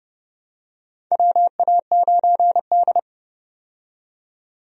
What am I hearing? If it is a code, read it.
Morse code, heard as a keyed tone, about 31 wpm, 700 Hz. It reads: WA9D
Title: USA Call Signs - 4 Characters - 30wpm